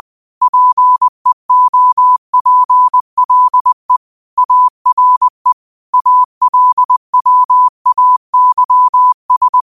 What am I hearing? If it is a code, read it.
Morse code, heard as a keyed tone, about 20 wpm, 1000 Hz. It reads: PEOPLE ARE ALWAYS
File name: qotdfiles / 20wpm